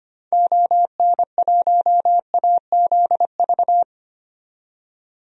Morse code, heard as a keyed tone, about 25 wpm, 700 Hz. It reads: ON1AZV
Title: DXCC-100 Call Signs - 6 Characters - 25wpm